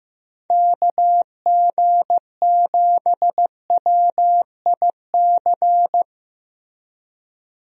Morse code, heard as a keyed tone, about 15 wpm, 700 Hz. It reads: KG7WIC
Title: USA Call Signs - 6 Characters - Repeated 2x - 15wpm